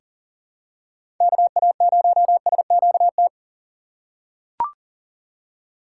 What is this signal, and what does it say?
Morse code, heard as a keyed tone, about 37 wpm, 700 Hz. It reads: KA0SQT
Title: USA Call Signs - 6 Characters - 4x Character Spacing - 40wpm